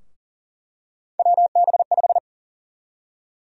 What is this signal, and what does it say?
Morse code, heard as a keyed tone, about 40 wpm, 700 Hz. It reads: WB5
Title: USA Call Sign Prefixes - 40wpm